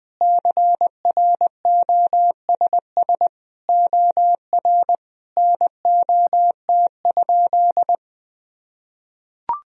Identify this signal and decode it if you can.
Morse code, heard as a keyed tone, about 21 wpm, 700 Hz. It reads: CROSS OR NOT?
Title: Sentences from Top 1000 Words - Review Part 3 - 20wpm